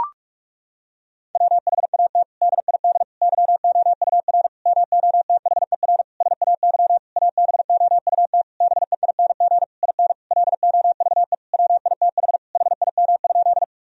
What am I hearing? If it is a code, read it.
Morse code, heard as a keyed tone, about 46 wpm, 700 Hz. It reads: WHAT DID YOUR MOTHER SAY ABOUT BEING IN LOVE WITH HIM?
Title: Sentences from Top 1000 Words - Review Part 3 - 45wpm